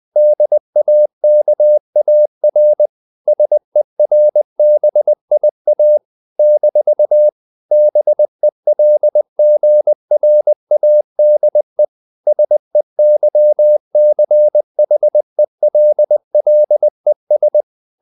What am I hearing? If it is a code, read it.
Morse code, heard as a keyed tone, about 20 wpm, 600 Hz. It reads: DAKAR SERBIA - BELGRADE SEYCHELLES
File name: CW / Practice / NationalCapitals20